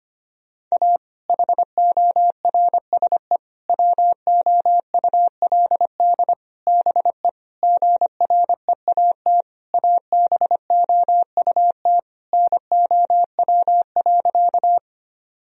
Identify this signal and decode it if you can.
Morse code, heard as a keyed tone, about 25 wpm, 700 Hz. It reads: A HORSE WOULD BE GREAT ABOUT NOW.
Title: Sentences from Top 1000 Words - Review Part 1 - 25wpm